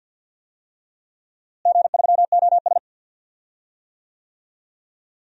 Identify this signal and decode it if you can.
Morse code, heard as a keyed tone, about 53 wpm, 700 Hz. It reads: G3OS
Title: DXCC-100 Call Signs - 4 Characters - Repeated 2x - 50wpm